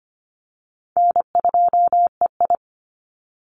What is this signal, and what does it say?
Morse code, heard as a keyed tone, about 25 wpm, 700 Hz. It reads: N2EI
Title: Easy Call Signs - 25wpm